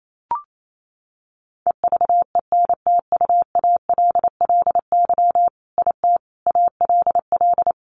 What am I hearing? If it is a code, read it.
Morse code, heard as a keyed tone, about 28 wpm, 700 Hz. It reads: EVENTUALLY IT ALL
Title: A Sound Of Thunder 28wpm